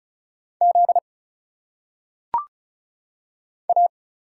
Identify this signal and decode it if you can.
Morse code, heard as a keyed tone, about 34 wpm, 700 Hz. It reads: Z A
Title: Single Letters - Phonetic Pronunciation 35wpm